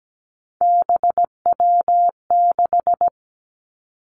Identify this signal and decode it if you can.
Morse code, heard as a keyed tone, about 17 wpm, 700 Hz. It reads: BW6
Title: Three Challenging Letter-Number Combinations 17wpm